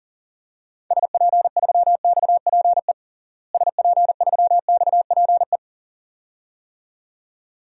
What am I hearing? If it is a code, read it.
Morse code, heard as a keyed tone, about 40 wpm, 700 Hz. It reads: SP3XPE SP3XPE
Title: DXCC-100 Call Signs - 6 Characters - Repeated 2x - 40wpm